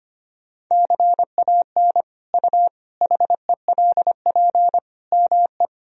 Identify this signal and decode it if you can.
Morse code, heard as a keyed tone, about 25 wpm, 700 Hz. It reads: CAN U HELP ME
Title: Sentences from Top 200 Words and Abbreviations 25wpm